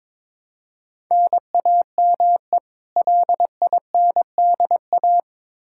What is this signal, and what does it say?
Morse code, heard as a keyed tone, about 22 wpm, 700 Hz. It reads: NAME LINDA
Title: Practice QSO - Exchange 1 - Beginning - 22wpm